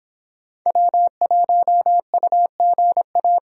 Morse code, heard as a keyed tone, about 25 wpm, 700 Hz. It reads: W1UGA